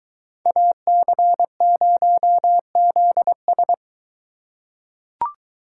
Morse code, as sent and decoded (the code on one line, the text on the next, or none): .- -.-. ----- --.. ...
AC0ZS